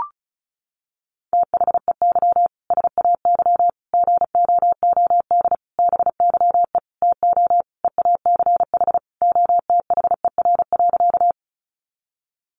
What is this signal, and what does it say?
Morse code, heard as a keyed tone, about 34 wpm, 700 Hz. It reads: THEY SAY GOOD BYE TO EACH OTHER.
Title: Easy Phrases 35wpm